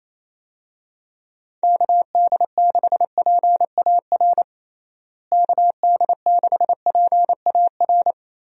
Morse code, heard as a keyed tone, about 28 wpm, 700 Hz. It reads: KD6PAR KD6PAR
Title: USA Call Signs - 6 Characters - Repeated 3x - 28wpm